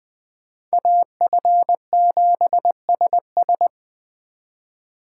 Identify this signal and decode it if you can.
Morse code, heard as a keyed tone, about 24 wpm, 700 Hz. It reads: AF7SS
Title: Speed Racing - USA 2x2 Call Signs 20wpm